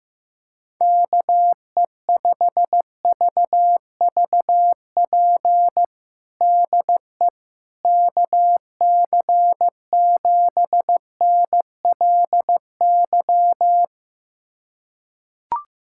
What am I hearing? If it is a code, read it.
Morse code, heard as a keyed tone, about 15 wpm, 700 Hz. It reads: KE5VVP DE KC7NLY
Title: USA Callsign DE Callsign 15wpm